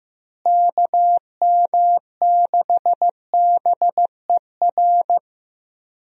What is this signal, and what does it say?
Morse code, heard as a keyed tone, about 16 wpm, 700 Hz. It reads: KM6BER
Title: USA Call Signs - 6 Characters - Repeated 3x - 15wpm